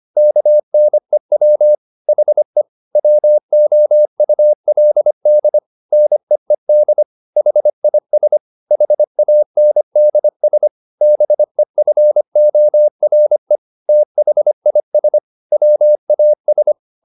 Morse code, heard as a keyed tone, about 25 wpm, 600 Hz. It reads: KNEW HE WOULD NEED HIS HANDS BEFORE THIS WAS
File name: the old man and the sea part 1 25wpm